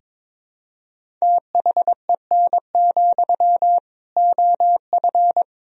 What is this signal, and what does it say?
Morse code, heard as a keyed tone, about 22 wpm, 700 Hz. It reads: THEN, OF
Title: Sentences from Top 1000 Words - Review Part 1 - 22wpm